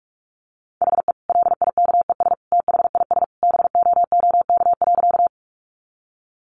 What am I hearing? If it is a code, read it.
Morse code, heard as a keyed tone, about 44 wpm, 700 Hz. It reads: HE LIKES THIS BOOK.
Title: Additional Easy Phrases 45wpm